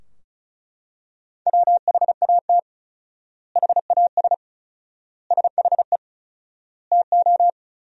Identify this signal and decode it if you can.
Morse code, heard as a keyed tone, about 34 wpm, 700 Hz. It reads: WHAT HAS SHE TO
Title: 4x Word-Spacing - Sentences from Top 100 Words - 35wpm